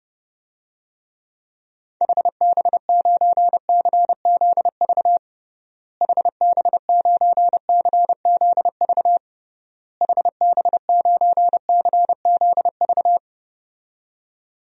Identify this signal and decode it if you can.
Morse code, heard as a keyed tone, about 30 wpm, 700 Hz. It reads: HB9CZV HB9CZV HB9CZV
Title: DXCC-100 Call Signs - 6 Characters - Repeated 3x - 30wpm